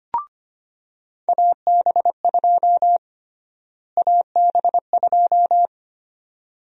Morse code, heard as a keyed tone, about 25 wpm, 700 Hz. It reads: AB2 AB2
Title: USA Call Sign Prefixes - Repeated 2x - 25wpm